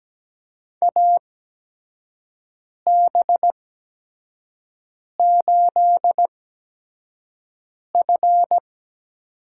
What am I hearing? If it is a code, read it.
Morse code, heard as a keyed tone, about 17 wpm, spaced as for 5 wpm, 700 Hz. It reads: AB8F
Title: USA Call Signs - 4 Characters - 8x Character Spacing - 17wpm